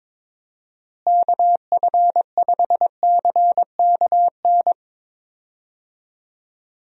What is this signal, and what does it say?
Morse code, heard as a keyed tone, about 22 wpm, 700 Hz. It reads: KF5CKN
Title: USA Call Signs - 6 Characters - Repeated 2x - 22wpm